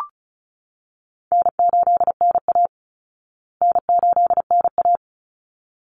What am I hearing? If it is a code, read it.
Morse code, heard as a keyed tone, about 34 wpm, 700 Hz. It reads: N8NA N8NA
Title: Easy Call Signs Repeated 2x - 35wpm